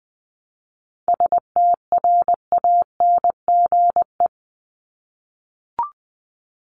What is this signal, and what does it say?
Morse code, heard as a keyed tone, about 20 wpm, 700 Hz. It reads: STRANGE
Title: Top 1501-1750 Words 20wpm